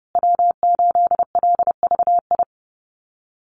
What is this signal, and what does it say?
Morse code, heard as a keyed tone, about 30 wpm, 700 Hz. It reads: W8LVI
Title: Call Signs Repeated 2x - 30wpm